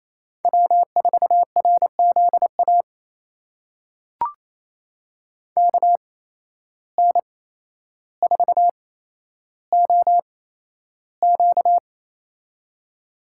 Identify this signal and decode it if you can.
Morse code, heard as a keyed tone, about 28 wpm, spaced as for 12 wpm, 700 Hz. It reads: W4RZA KN4OQ
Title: USA Call Signs - 5 Characters - 8x Character Spacing - 28wpm